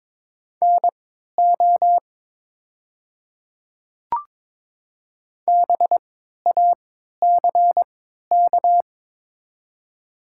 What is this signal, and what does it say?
Morse code, heard as a keyed tone, about 22 wpm, spaced as for 12 wpm, 700 Hz. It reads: NO BACK
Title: 3x Character Spacing - Top 300 Words - 22wpm